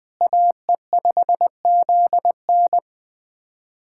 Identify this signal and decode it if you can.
Morse code, heard as a keyed tone, about 20 wpm, 700 Hz. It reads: AE5ZN
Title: USA Callsign DE Callsign 20wpm